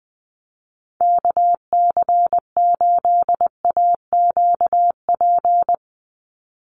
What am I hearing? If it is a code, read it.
Morse code, heard as a keyed tone, about 20 wpm, 700 Hz. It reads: KC8AQP
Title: USA Call Signs 20wpm